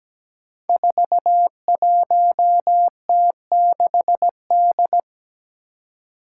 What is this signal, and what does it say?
Morse code, heard as a keyed tone, about 17 wpm, 700 Hz. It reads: 41T6D
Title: Five Challenging Letter-Number Combinations 17wpm